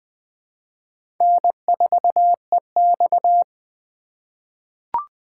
Morse code, heard as a keyed tone, about 21 wpm, 700 Hz. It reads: N4EX
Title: 2023 POTA Hunter Call Signs - 20wpm